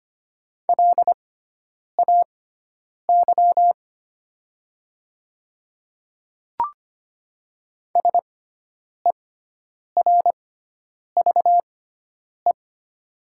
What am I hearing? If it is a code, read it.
Morse code, heard as a keyed tone, about 25 wpm, spaced as for 9 wpm, 690 Hz. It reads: LAY SERVE
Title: 6x Character Spacing - Top 500 Words - 25wpm